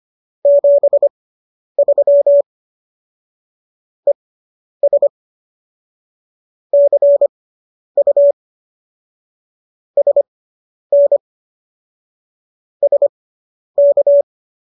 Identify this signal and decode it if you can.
Morse code, heard as a keyed tone, about 25 wpm, spaced as for 10 wpm, 570 Hz. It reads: 73 ES CU SN SK